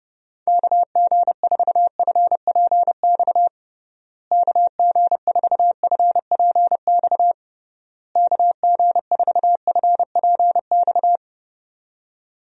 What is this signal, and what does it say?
Morse code, heard as a keyed tone, about 32 wpm, 700 Hz. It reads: KG4FPX KG4FPX KG4FPX
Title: USA Call Signs - 6 Characters - Repeated 3x - 30wpm